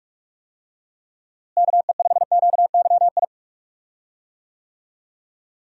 Morse code, heard as a keyed tone, about 46 wpm, 700 Hz. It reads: KE5QYI
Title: USA Call Signs - 6 Characters - Repeated 2x - 45wpm